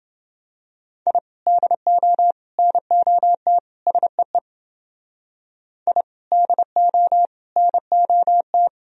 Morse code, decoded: I DO NOT SEE I DO NOT

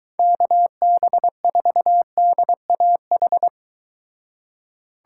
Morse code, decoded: KB4DAH

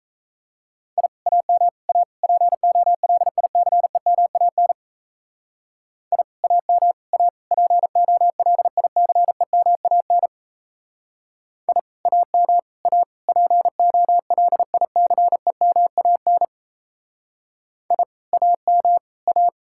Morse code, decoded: I AM A POLICEMAN I AM A POLICEMAN I AM A POLICEMAN I AM A

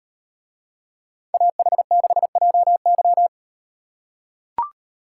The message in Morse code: AH6JY

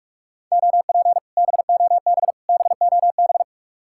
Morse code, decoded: OP BOB BOB

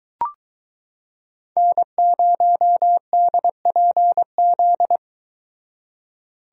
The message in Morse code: N0DPZ

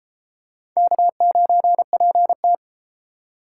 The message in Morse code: K9PT

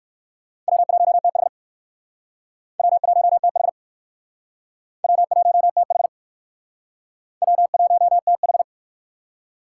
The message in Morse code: W1TH W1TH W1TH W1TH

